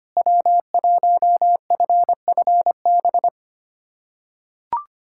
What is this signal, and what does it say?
Morse code, heard as a keyed tone, about 25 wpm, 700 Hz. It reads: W1FFB
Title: USA Callsign DE Callsign - 5 Characters - 25wpm